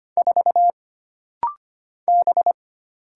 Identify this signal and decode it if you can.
Morse code, heard as a keyed tone, about 25 wpm, 700 Hz. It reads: 4 B
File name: Single Character (Letter:Number:Punctuation:Prosign) - Rapid-Fire - 25wpm